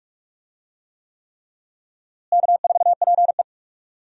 Morse code, heard as a keyed tone, about 42 wpm, 700 Hz. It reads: K4PE
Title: USA Call Signs - 4 Characters - Repeated 3x - 45wpm